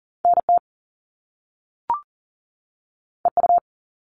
Morse code, heard as a keyed tone, about 42 wpm, 700 Hz. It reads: NT EU
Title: Top Two Letter Combinations in Words 40wpm